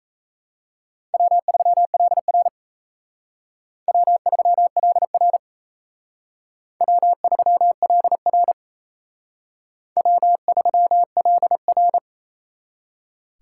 Morse code, heard as a keyed tone, about 35 wpm, 700 Hz. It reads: W3LR W3LR W3LR W3LR
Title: Speed Racing - USA 1x2 Call Signs 28wpm